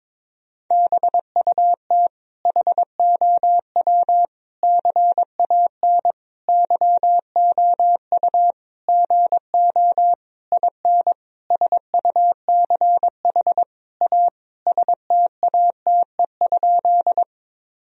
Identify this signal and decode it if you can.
Morse code, heard as a keyed tone, about 22 wpm, 700 Hz. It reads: BUT HOW CAN YOU GO IN SUCH A STATE?
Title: Sentences from Top 1000 Words - Review Part 2 - 22wpm